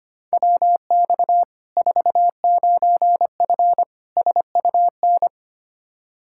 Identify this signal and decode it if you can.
Morse code, heard as a keyed tone, about 25 wpm, 700 Hz. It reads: WX 49F SUN
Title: Practice QSO - Exchange 2 - Info - 25wpm